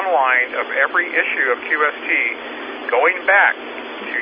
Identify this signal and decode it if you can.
Emergency Alert System (SAME) tones, heard as harsh buzzing bursts, with speech. It is heard from 0.7 to 3.6 s.